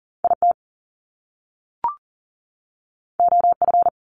Morse code, heard as a keyed tone, about 42 wpm, 700 Hz. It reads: IT OF
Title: Top Two Letter Combinations in Words 40wpm